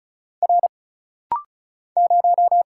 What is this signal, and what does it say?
Morse code, heard as a keyed tone, about 34 wpm, 700 Hz. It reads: R 0
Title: Single Letter-Number Alt - Rapid-Fire 35wpm